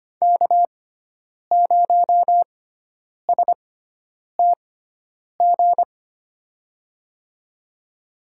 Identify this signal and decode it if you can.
Morse code, heard as a keyed tone, about 25 wpm, spaced as for 9 wpm, 700 Hz. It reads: K0STG
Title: USA Call Signs - 5 Characters - 6x Character Spacing - 25wpm